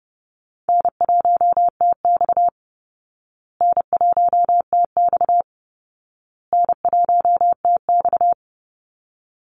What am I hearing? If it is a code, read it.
Morse code, heard as a keyed tone, about 30 wpm, 700 Hz. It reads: N1TX N1TX N1TX
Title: Easy Call Signs Repeated 3x - 30wpm